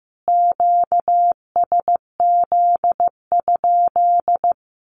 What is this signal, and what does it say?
Morse code, heard as a keyed tone, about 15 wpm, 700 Hz. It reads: QSZ?
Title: Q-Codes Repeated 3x 15wpm